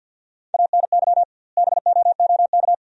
Morse code, heard as a keyed tone, about 50 wpm, 700 Hz. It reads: ANY BOOK
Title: 2x Word-Spacing - Sentences from Top 300 Words - 50wpm